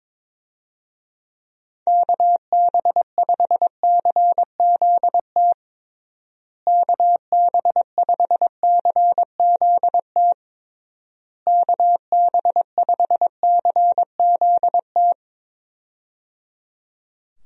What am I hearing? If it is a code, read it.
Morse code, heard as a keyed tone, about 22 wpm, 700 Hz. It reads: KB5CZT KB5CZT KB5CZT
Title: USA Call Signs - 6 Characters - Repeated 3x - 22wpm